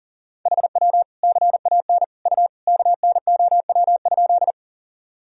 Morse code, heard as a keyed tone, about 40 wpm, 700 Hz. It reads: HW CAN U KNOW?
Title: Sentences from Top 100 Words and Abbreviations 40wpm